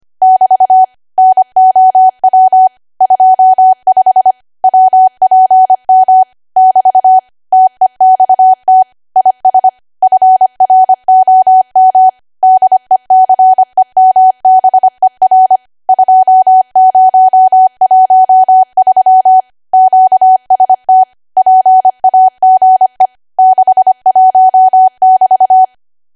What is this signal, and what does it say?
Morse code, heard as a keyed tone, about 25 wpm, 750 Hz. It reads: <BT> NOW 25 WPM <BT> TEXT IS FROM DECEMBER 2013 QST PAGE 61<BT>